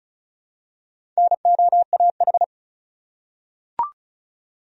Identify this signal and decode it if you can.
Morse code, heard as a keyed tone, about 35 wpm, 700 Hz. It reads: NOAH
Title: USA Common Names - Max 4 Letters - 35wpm